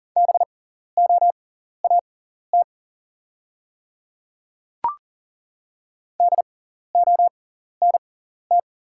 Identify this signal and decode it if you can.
Morse code, heard as a keyed tone, about 40 wpm, spaced as for 14 wpm, 690 Hz. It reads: BOAT DONT
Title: 6x Character Spacing - Top 500 Words - 40wpm